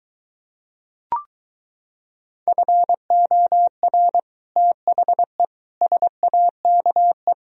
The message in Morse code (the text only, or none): FOR THE SAKE